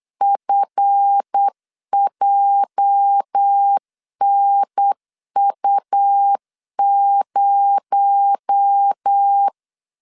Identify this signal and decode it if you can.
Morse code, heard as a keyed tone, about 8 wpm, 800 Hz. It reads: FJNU0
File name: LEZIONI / 8 YJ45 / 8 YJ 8wpm 3313